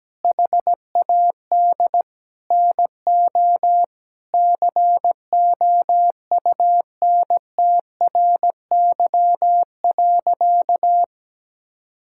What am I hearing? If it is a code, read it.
Morse code, heard as a keyed tone, about 17 wpm, 700 Hz. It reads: HAD NO COUNTRY.
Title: Sentences from Top 1000 Words - Review Part 5 - 17wpm